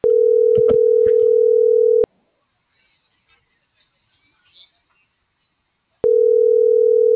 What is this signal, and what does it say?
Ringback tone (North America), 440 + 480 Hz.